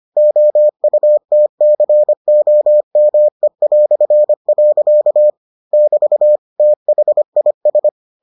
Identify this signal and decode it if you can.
Morse code, heard as a keyed tone, about 25 wpm, 600 Hz. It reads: OUTCOME". <BT> THIS